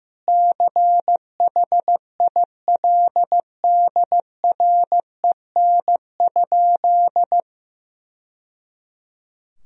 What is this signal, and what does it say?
Morse code, heard as a keyed tone, about 15 wpm, 700 Hz. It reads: CHILDREN?
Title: Sentences from Top 1000 Words - Review Part 2 - 15wpm